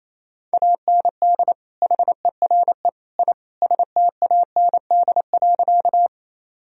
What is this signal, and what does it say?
Morse code, heard as a keyed tone, about 27 wpm, 700 Hz. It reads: AND HERE I STAND.
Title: Sentences from Top 1000 Words - Review Part 1 - 28wpm